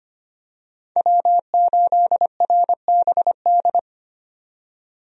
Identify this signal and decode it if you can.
Morse code, heard as a keyed tone, about 25 wpm, 690 Hz. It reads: W8RBD
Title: USA Call Signs - 5 Characters - 25wpm